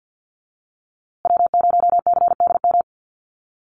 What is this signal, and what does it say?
Morse code, heard as a keyed tone, about 50 wpm, 700 Hz. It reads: W0CDM
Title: USA Call Signs Repeated 3x 50wpm